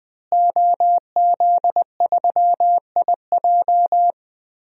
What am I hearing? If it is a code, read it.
Morse code, heard as a keyed tone, about 20 wpm, 700 Hz. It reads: OZ3IJ
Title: DXCC-100 Call Signs - 5 Characters - Repeated 3x - 20wpm